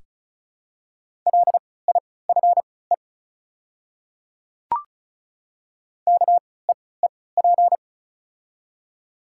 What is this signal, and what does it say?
Morse code, heard as a keyed tone, about 36 wpm, spaced as for 20 wpm, 700 Hz. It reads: LIFE KEEP
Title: 3x Character Spacing - Top 300 Words - 35wpm